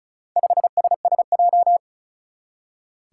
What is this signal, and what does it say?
Morse code, heard as a keyed tone, about 36 wpm, 700 Hz. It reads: 5SSJ
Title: Four Challenging Letter-Number Combinations 35wpm